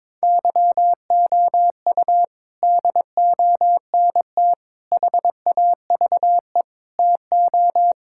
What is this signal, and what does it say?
Morse code, heard as a keyed tone, about 22 wpm, 700 Hz. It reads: YOU DONT HAVE TO
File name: practice / 22wpm / Sets of 6 Words - 22wpm